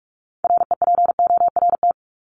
Morse code, heard as a keyed tone, about 44 wpm, 700 Hz. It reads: REPORT
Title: ED Suffix - Top 100 Words with Base Words 45wpm